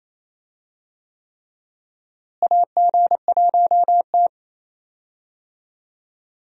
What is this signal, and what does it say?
Morse code, heard as a keyed tone, about 28 wpm, 700 Hz. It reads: AG1T
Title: USA Call Signs - 4 Characters - Repeated 3x - 28wpm